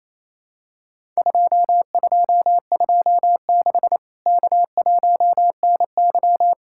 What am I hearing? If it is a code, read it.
Morse code, heard as a keyed tone, about 28 wpm, 700 Hz. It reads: 2226 K1NY
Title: Worked All Europe DX Contest - QTC Exchange - 28wpm